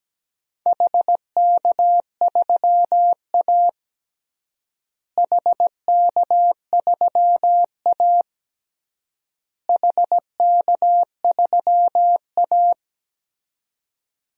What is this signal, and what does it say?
Morse code, heard as a keyed tone, about 17 wpm, 700 Hz. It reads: HK3A HK3A HK3A
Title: DXCC-100 Call Signs - 4 Characters - Repeated 3x - 17wpm